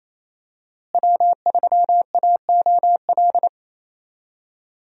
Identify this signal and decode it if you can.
Morse code, heard as a keyed tone, about 28 wpm, 700 Hz. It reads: W3AOL
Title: USA Call Signs - 5 Characters - Repeated 2x - 28wpm